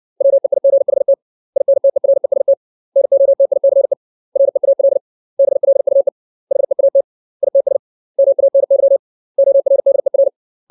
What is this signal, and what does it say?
Morse code, heard as a keyed tone, about 60 wpm, 550 Hz. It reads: WEIGHT INTEREST NOTICE REAL BLUE HEAT ITS MANY ORDER
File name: 632 words / 632 Words 60WPM